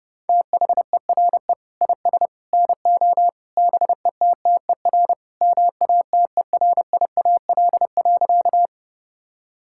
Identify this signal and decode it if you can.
Morse code, heard as a keyed tone, about 30 wpm, 700 Hz. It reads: THERE IS NO BETTER MATERIAL.
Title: Sentences from Top 1000 Words - Review Part 5 - 30wpm